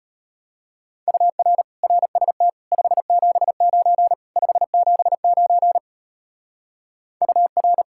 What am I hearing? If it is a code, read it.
Morse code, heard as a keyed tone, about 32 wpm, 700 Hz. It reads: UR RST 579 579 UR